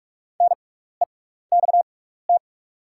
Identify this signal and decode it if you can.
Morse code, heard as a keyed tone, about 44 wpm, spaced as for 15 wpm, 700 Hz. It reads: NEXT
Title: 6x Character Spacing - Top 500 Words - 45wpm